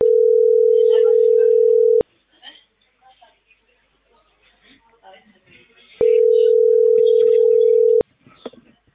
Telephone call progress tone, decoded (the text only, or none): ringback tone